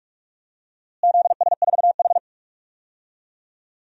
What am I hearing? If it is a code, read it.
Morse code, heard as a keyed tone, about 41 wpm, 700 Hz. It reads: ZS4H
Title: DXCC-100 Call Signs - 4 Characters - Repeated 3x - 45wpm